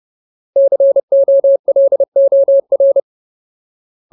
Morse code, heard as a keyed tone, about 30 wpm, 550 Hz. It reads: COLOR
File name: Ultimate Word List / 30WPM Ultimate